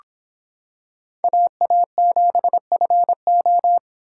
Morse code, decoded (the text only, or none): AA7FO